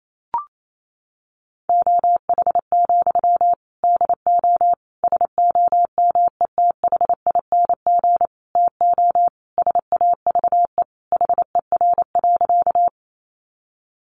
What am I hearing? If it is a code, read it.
Morse code, heard as a keyed tone, about 28 wpm, 700 Hz. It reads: OH, DO SOMETHING TO SAVE HER.
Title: Sentences from Top 1100 Words 28wpm